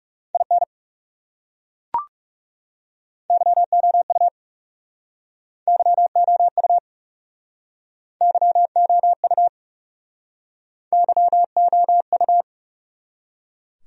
Morse code, heard as a keyed tone, about 40 wpm, 700 Hz. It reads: IN YOU YOU YOU YOU